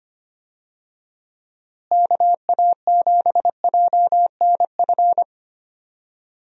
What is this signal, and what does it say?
Morse code, heard as a keyed tone, about 25 wpm, 700 Hz. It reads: KA7JNF
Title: USA Call Signs - 6 Characters - Repeated 2x - 25wpm